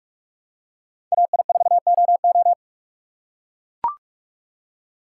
Morse code, heard as a keyed tone, about 43 wpm, 700 Hz. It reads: AI4OO